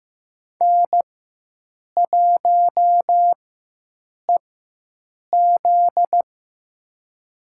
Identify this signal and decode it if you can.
Morse code, heard as a keyed tone, about 15 wpm, spaced as for 7 wpm, 700 Hz. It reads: N1EZ
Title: USA Call Signs - 4 Characters - 4x Character Spacing - 15wpm